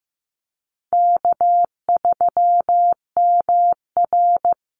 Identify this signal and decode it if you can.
Morse code, heard as a keyed tone, about 15 wpm, 700 Hz. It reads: K3MR
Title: Call Signs Repeated 3x - 15wpm